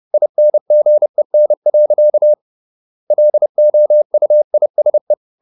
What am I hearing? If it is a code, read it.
Morse code, heard as a keyed tone, about 30 wpm, 600 Hz. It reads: INGEN. LOUISE
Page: Ferie 30 wpm